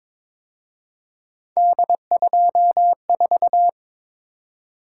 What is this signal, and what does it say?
Morse code, heard as a keyed tone, about 26 wpm, 700 Hz. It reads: D24
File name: Speed-Racing - Three Letter-Number Combinations - 22wpm